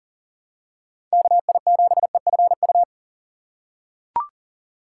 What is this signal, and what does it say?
Morse code, heard as a keyed tone, about 40 wpm, 700 Hz. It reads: KI7EFU
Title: USA Call Signs - 6 Characters - 4x Character Spacing - 40wpm